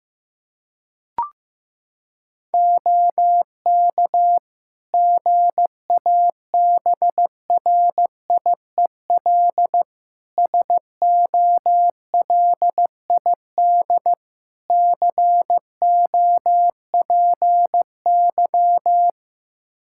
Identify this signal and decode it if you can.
Morse code, heard as a keyed tone, about 15 wpm, 700 Hz. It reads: OK GABRIEL SOLID COPY
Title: Practice QSO - Exchange 3 - More Info - 15wpm